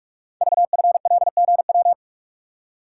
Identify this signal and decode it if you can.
Morse code, heard as a keyed tone, about 44 wpm, 700 Hz. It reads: VFLGW
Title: Code Groups - 5 Letters - 45wpm